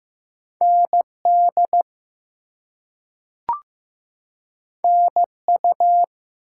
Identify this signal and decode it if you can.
Morse code, heard as a keyed tone, about 15 wpm, 700 Hz. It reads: ND NU
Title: US States and Canadian Provinces 15wpm